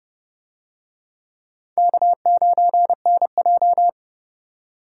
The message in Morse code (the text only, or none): K9NJ